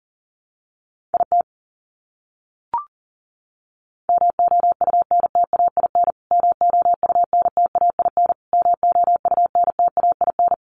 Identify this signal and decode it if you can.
Morse code, heard as a keyed tone, about 40 wpm, 700 Hz. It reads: IT MOUNTAIN MOUNTAIN MOUNTAIN
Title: Top 500 Words Repeated 40wpm